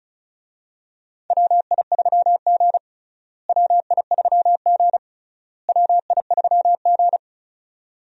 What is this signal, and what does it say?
Morse code, heard as a keyed tone, about 34 wpm, 700 Hz. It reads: WI3G WI3G WI3G
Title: USA Call Signs - 4 Characters - Repeated 3x - 35wpm